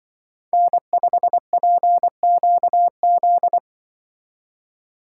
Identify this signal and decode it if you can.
Morse code, heard as a keyed tone, about 24 wpm, 700 Hz. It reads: N5PQZ
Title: Speed Racing - USA 1x3 Call Signs 20wpm